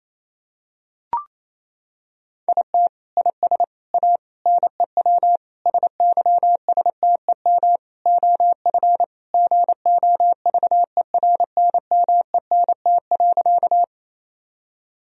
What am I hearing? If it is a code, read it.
Morse code, heard as a keyed tone, about 28 wpm, 700 Hz. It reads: IT IS A NEW SYSTEM OF GOVERNMENT.
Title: Sentences from Top 1000 Words - Review Part 3 - 28wpm